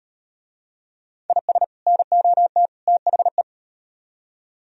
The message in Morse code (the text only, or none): IS NOT THE